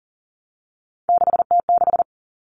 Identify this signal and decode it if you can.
Morse code, heard as a keyed tone, about 40 wpm, 700 Hz. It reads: 6T6
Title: All Cut Numbers - 3-Digit 40wpm